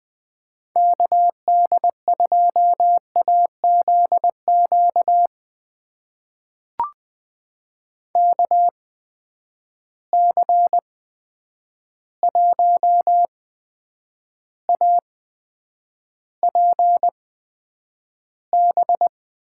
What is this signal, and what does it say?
Morse code, heard as a keyed tone, about 20 wpm, spaced as for 9 wpm, 700 Hz. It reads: KD2AZQ KC1APB
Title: USA Call Signs - 6 Characters - 8x Character Spacing - 20wpm